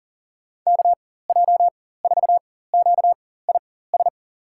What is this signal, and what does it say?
Morse code, heard as a keyed tone, about 40 wpm, spaced as for 19 wpm, 700 Hz. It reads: KJ4QIS
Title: USA Call Signs - 4x Character Spacing - 40wpm